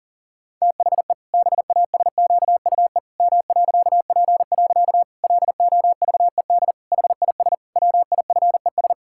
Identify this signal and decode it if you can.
Morse code, heard as a keyed tone, about 40 wpm, 700 Hz. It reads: THE BASQUE M.P. LOVED HIS WIFES
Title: Pangram Challenge v2 - 40wpm